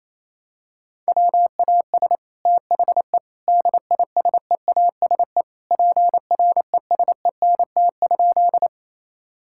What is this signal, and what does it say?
Morse code, heard as a keyed tone, about 28 wpm, 700 Hz. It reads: WAS THE DISEASE PRESENT?
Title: Sentences from Top 1000 Words - Review Part 6 - 28wpm